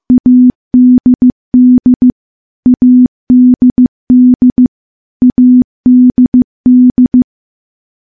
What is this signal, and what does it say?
Morse code, heard as a keyed tone, about 15 wpm, 250 Hz. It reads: ADD ADD ADD
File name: morse-training-15WPM